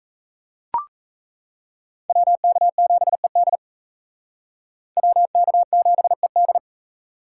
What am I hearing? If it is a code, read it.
Morse code, heard as a keyed tone, about 36 wpm, 700 Hz. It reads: WK7ED WK7ED